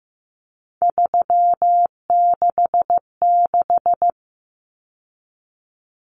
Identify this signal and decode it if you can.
Morse code, heard as a keyed tone, about 15 wpm, 700 Hz. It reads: 366
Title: Common Cut Numbers - 3-Digit 15wpm